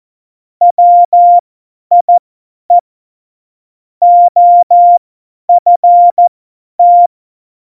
Morse code, heard as a keyed tone, about 14 wpm, spaced as for 10 wpm, 700 Hz. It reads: WIE OFT
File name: Morsekurs / Witze / Witz 18. September 2018, 50 F70